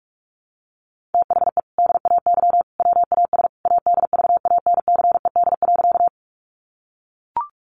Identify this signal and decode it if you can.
Morse code, heard as a keyed tone, about 44 wpm, 700 Hz. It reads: THE DAY WAS ADVANCED.
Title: Sentences from Top 1200 Words 45wpm